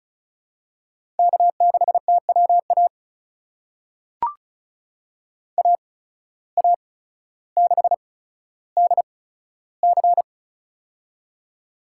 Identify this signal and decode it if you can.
Morse code, heard as a keyed tone, about 36 wpm, spaced as for 15 wpm, 700 Hz. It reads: K6TWA AA6DC
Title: USA Call Signs - 8x Character Spacing - 35wpm